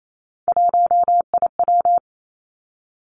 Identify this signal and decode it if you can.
Morse code, heard as a keyed tone, about 28 wpm, 700 Hz. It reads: 1IW
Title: Three Challenging Letter-Number Combinations 28wpm